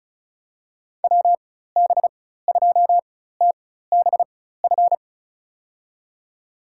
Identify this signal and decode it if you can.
Morse code, heard as a keyed tone, about 34 wpm, spaced as for 16 wpm, 700 Hz. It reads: WB2TBF
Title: USA Call Signs - 6 Characters - 4x Character Spacing - 35wpm